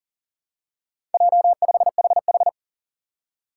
Morse code, heard as a keyed tone, about 40 wpm, 700 Hz. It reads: J5HH
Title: Four Challenging Letter-Number Combinations 40wpm